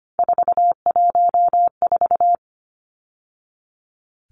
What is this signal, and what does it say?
Morse code, heard as a keyed tone, about 25 wpm, 700 Hz. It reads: 414
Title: Numbers - 3-Digit 25wpm